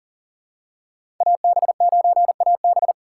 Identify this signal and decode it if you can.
Morse code, heard as a keyed tone, about 40 wpm, 700 Hz. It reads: AB9AB
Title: USA Call Signs - 5 Characters - 4x Character Spacing - 40wpm